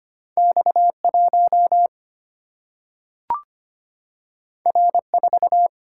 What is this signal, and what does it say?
Morse code, heard as a keyed tone, about 25 wpm, 700 Hz. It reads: X1 R4
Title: Single-Letter Single-Number 25wpm